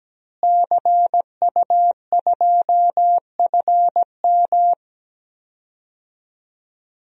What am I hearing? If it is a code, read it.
Morse code, heard as a keyed tone, about 17 wpm, 700 Hz. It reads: CU2FM
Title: DXCC-100 Call Signs - 5 Characters - Repeated 3x - 17wpm